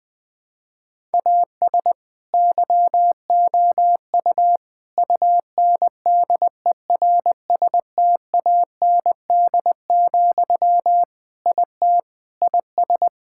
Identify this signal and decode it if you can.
Morse code, heard as a keyed tone, about 20 wpm, 700 Hz. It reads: AS YOU UNDERSTAND, IT IS
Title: Sentences from Top 1000 Words - Review Part 5 - 20wpm